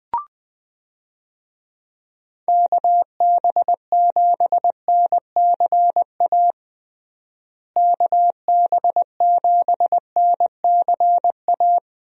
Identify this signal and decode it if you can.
Morse code, heard as a keyed tone, about 20 wpm, 700 Hz. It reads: KB7NCA KB7NCA
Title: USA Call Signs - 6 Characters - Repeated 2x - 20wpm